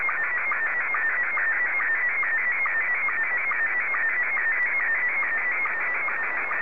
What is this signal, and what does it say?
SSTV (slow-scan television), Scottie 1.